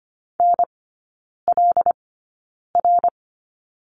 Morse code, heard as a keyed tone, about 25 wpm, spaced as for 9 wpm, 700 Hz. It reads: NLR
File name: course / 25:10wpm / 093 Lesson 16e - Letter c - Three Characters 25wpm